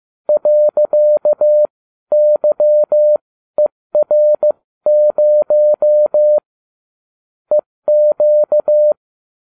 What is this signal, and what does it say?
Morse code, heard as a keyed tone, about 15 wpm, spaced as for 13 wpm, 600 Hz. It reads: .YER0 EQ